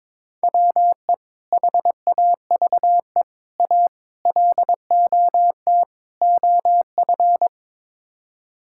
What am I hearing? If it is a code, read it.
Morse code, heard as a keyed tone, about 22 wpm, 700 Hz. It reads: WE HAVE A LOT OF
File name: practice / 22wpm / Build A Phrase - 5-Word Phrases - 22wpm